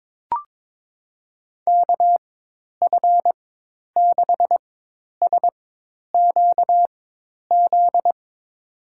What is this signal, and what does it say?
Morse code, heard as a keyed tone, about 22 wpm, spaced as for 10 wpm, 700 Hz. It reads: KF6SQZ
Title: USA Call Signs - 6 Characters - 4x Character Spacing - 22wpm